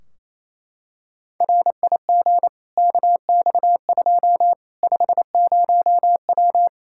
Morse code, heard as a keyed tone, about 28 wpm, 700 Hz. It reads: RIG KX2 50W
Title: K1USN SST QSO Practice 28wpm